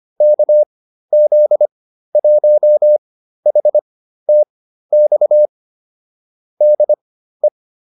Morse code, heard as a keyed tone, about 25 wpm, spaced as for 13 wpm, 600 Hz. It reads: KZ1HTX DE
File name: QSO301 13